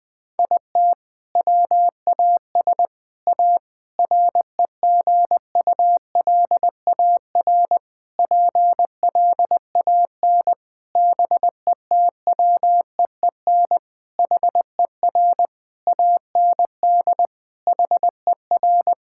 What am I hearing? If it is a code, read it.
Morse code, heard as a keyed tone, about 20 wpm, 700 Hz. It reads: IT WAS A REGULAR PLAN BETWEEN HER AND HER
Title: Sentences from Top 1000 Words - Review Part 6 - 20wpm